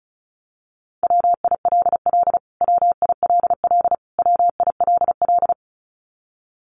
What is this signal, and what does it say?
Morse code, heard as a keyed tone, about 34 wpm, 700 Hz. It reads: WILL WILL WILL
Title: Top 500 Words Repeated 35wpm